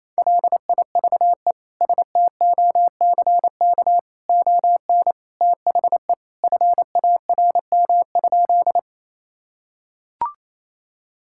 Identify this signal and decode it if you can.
Morse code, heard as a keyed tone, about 28 wpm, 700 Hz. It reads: LIVESTOCK ON THE FARM?
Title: Sentences from Top 1000 Words - Review Part 5 - 28wpm